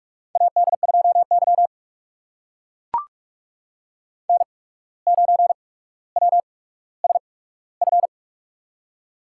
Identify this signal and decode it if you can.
Morse code, heard as a keyed tone, about 44 wpm, spaced as for 18 wpm, 700 Hz. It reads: AD2Y N9WSF
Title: USA Call Signs - 8x Character Spacing - 45wpm